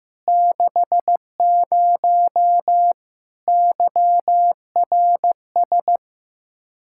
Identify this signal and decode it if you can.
Morse code, heard as a keyed tone, about 15 wpm, 700 Hz. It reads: 60 YRS
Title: Practice QSO - Exchange 3 - More Info - 15wpm